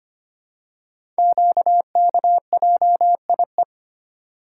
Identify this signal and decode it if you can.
Morse code, heard as a keyed tone, about 25 wpm, 700 Hz. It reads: QKJIE
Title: Code Groups - 5 Letters - 25wpm